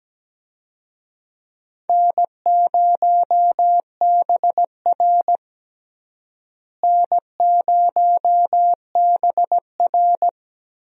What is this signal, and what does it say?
Morse code, heard as a keyed tone, about 17 wpm, 700 Hz. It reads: N0BR N0BR
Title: USA Call Signs - 4 Characters - Repeated 3x - 17wpm